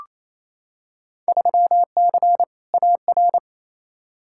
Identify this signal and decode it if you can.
Morse code, heard as a keyed tone, about 28 wpm, 700 Hz. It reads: 3C AR
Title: Field Day Exchanges Encore - 28wpm